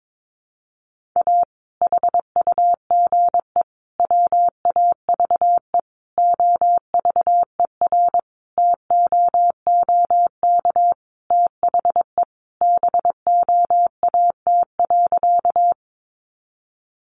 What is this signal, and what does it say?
Morse code, heard as a keyed tone, about 22 wpm, 700 Hz. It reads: A HUGE WAVE OVERTOOK THE BOAT.
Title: Sentences from Top 1400 Words 22wpm